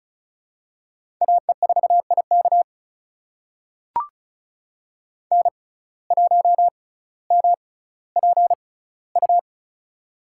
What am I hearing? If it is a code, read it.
Morse code, heard as a keyed tone, about 36 wpm, spaced as for 18 wpm, 700 Hz. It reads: AE4IK N1MPU
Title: USA Call Signs - 5 Characters - 6x Character Spacing - 35wpm